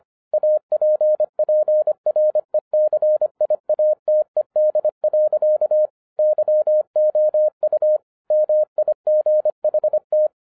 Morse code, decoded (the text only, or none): APPRECIATED. YOU MIGHT